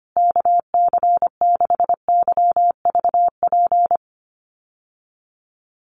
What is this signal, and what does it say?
Morse code, heard as a keyed tone, about 25 wpm, 700 Hz. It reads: KC6YVP
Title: USA Call Signs Repeated 3x 25wpm